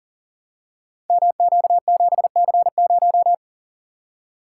Morse code, heard as a keyed tone, about 41 wpm, 690 Hz. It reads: MQ7C0
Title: Code Groups - 5 Letter:Numbers - 40wpm